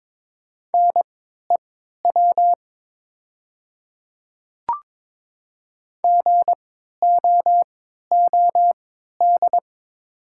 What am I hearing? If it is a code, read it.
Morse code, heard as a keyed tone, about 22 wpm, spaced as for 12 wpm, 700 Hz. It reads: NEW GOOD
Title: 3x Character Spacing - Top 300 Words - 22wpm